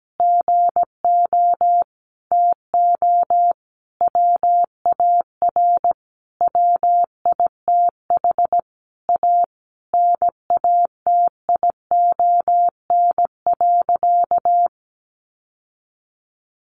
Morse code, heard as a keyed tone, about 17 wpm, 700 Hz. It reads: GO TO WAR WITH A NATION.
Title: Sentences from Top 1200 Words 17wpm